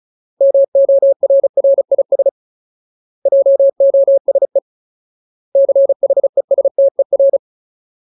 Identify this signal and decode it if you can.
Morse code, heard as a keyed tone, about 34 wpm, 550 Hz. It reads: MORRIS JOSE CHESTER
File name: Ultimate Word List / 35WPM Ultimate